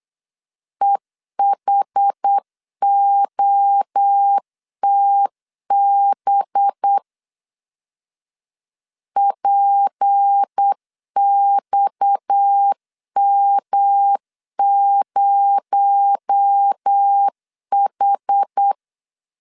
Morse code, heard as a keyed tone, about 8 wpm, 800 Hz. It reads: EHOTB PXM0H
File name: LEZIONI / 6 PKX0 / 6 PKX0 8wpm 3341